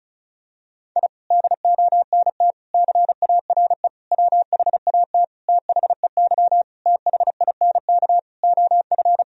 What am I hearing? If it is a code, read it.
Morse code, heard as a keyed tone, about 36 wpm, 700 Hz. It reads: I DONT CARE WHAT THEY THINK OF
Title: Sentences from Top 1000 Words - Review Part 1 - 35wpm